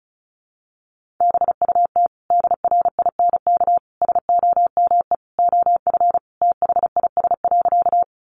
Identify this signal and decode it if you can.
Morse code, heard as a keyed tone, about 34 wpm, 700 Hz. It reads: BUT DRINK SOME OF THIS.
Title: Sentences from Top 1500 Words 35wpm